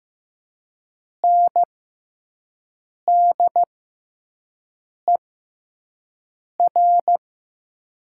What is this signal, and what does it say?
Morse code, heard as a keyed tone, about 15 wpm, spaced as for 5 wpm, 700 Hz. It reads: NDER
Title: 6x Character Spacing - Top 300 Words - 15wpm